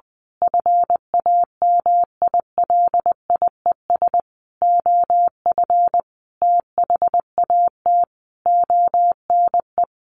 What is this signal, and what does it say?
Morse code, heard as a keyed tone, about 20 wpm, 700 Hz. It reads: FAMILIES OF THAT ONE
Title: A Sound Of Thunder 20wpm